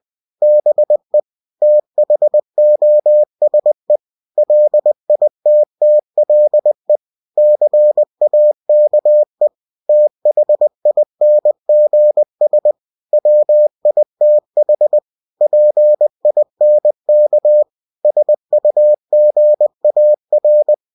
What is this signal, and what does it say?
Morse code, heard as a keyed tone, about 20 wpm, 600 Hz. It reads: BE THOSE LITTLE CAKE THINGS WITH PINK SUGAR
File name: cw-books / winnie-the-pooh / winnie-20wpm-page-0102